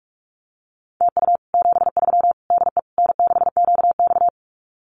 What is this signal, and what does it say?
Morse code, heard as a keyed tone, about 44 wpm, 700 Hz. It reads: TU 73 DE N6QX
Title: Common QSO Phrases Encore 45wpm